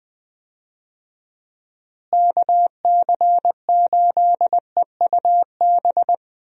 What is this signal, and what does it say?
Morse code, heard as a keyed tone, about 20 wpm, 700 Hz. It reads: KC8EUB
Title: USA Call Signs - 6 Characters - Repeated 2x - 20wpm